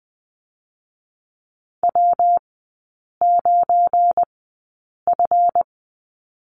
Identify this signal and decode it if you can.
Morse code, heard as a keyed tone, about 20 wpm, spaced as for 8 wpm, 690 Hz. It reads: W9F
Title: 172 Lesson 28e - Number 9 - Three Characters 20wpm